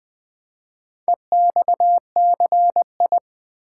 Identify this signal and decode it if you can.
Morse code, heard as a keyed tone, about 20 wpm, 690 Hz. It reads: EXCI
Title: Build A Word - 2 Letters At A Time - 8-Letter Words - 20wpm